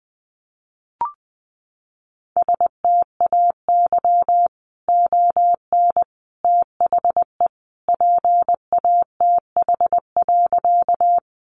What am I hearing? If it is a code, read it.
Morse code, heard as a keyed tone, about 20 wpm, 700 Hz. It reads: STAY ON THE PATH.
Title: A Sound Of Thunder 20wpm